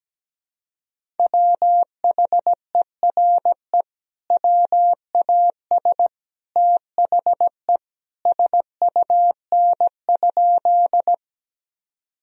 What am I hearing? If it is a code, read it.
Morse code, heard as a keyed tone, about 17 wpm, 700 Hz. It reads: WHERE WAS THE SUN?
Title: Sentences from Top 1000 Words - Review Part 1 - 17wpm